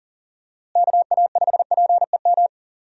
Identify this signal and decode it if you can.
Morse code, heard as a keyed tone, about 40 wpm, 700 Hz. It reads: KA5PEM
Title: USA Call Signs - 6 Characters - Repeated 2x - 40wpm